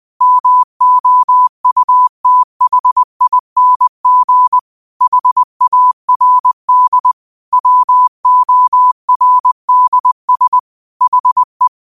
Morse code, heard as a keyed tone, about 20 wpm, 1000 Hz. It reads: MOUTHING HARD WORDS HE